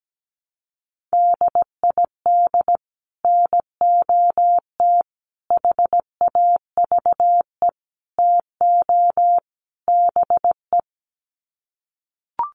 Morse code, heard as a keyed tone, about 17 wpm, 700 Hz. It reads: DID NOT HAVE TO BE
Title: Sets of 5 Words Encore2 17wpm